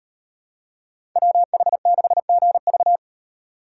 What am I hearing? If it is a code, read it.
Morse code, heard as a keyed tone, about 38 wpm, 700 Hz. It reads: WH6GV